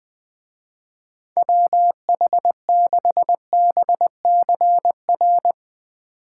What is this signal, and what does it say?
Morse code, heard as a keyed tone, about 20 wpm, 700 Hz. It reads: WH6BCR
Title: Speed Racing - USA 2x3 Call Signs 20wpm